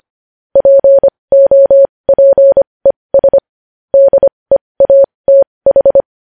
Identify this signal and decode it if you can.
Morse code, heard as a keyed tone, about 25 wpm, spaced as for 20 wpm, 550 Hz. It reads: POPES DEATH
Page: News Headlines in Morse Code at 20 WPM